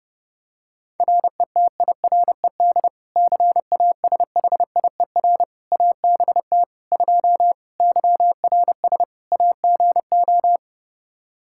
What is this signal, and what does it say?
Morse code, heard as a keyed tone, about 30 wpm, 700 Hz. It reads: RETIRED CASHIER ABT 2 YRS AGO
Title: K1USN SST QSO Practice 30wpm